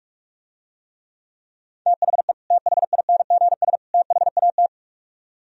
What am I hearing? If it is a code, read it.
Morse code, heard as a keyed tone, about 47 wpm, 700 Hz. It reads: THE THINGS THAT